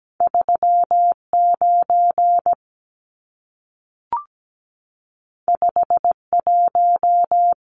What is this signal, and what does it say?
Morse code, heard as a keyed tone, about 17 wpm, 700 Hz. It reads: 39 51
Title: Numbers - 2-Digit 17wpm